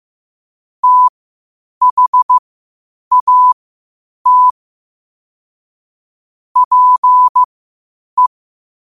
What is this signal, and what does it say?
Morse code, heard as a keyed tone, about 15 wpm, spaced as for 8 wpm, 1000 Hz. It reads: THAT PE
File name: qotdfiles / 8wpm